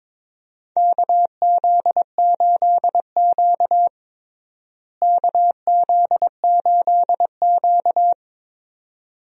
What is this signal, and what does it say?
Morse code, heard as a keyed tone, about 22 wpm, 700 Hz. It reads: KZ8Q KZ8Q
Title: USA Call Signs - 4 Characters - Repeated 2x - 22wpm